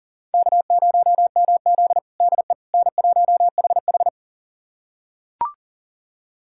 Morse code, heard as a keyed tone, about 40 wpm, 700 Hz. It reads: K0MZ DE N1HH
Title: USA Callsign DE Callsign - 4 Characters - 40wpm